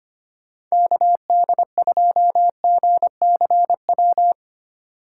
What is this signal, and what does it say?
Morse code, heard as a keyed tone, about 25 wpm, 700 Hz. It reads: KD2GCW
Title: USA Call Signs - 6 Characters - 4x Character Spacing - 25wpm